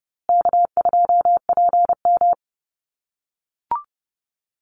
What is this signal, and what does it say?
Morse code, heard as a keyed tone, about 28 wpm, 700 Hz. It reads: K2PM
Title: USA Call Signs Repeated 3x 30wpm